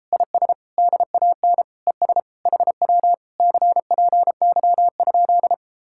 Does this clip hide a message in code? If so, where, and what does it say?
Yes: Morse code from 0.1 to 5.5 s, heard as a keyed tone, about 33 wpm, 700 Hz. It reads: IS DAN ES HW CPY?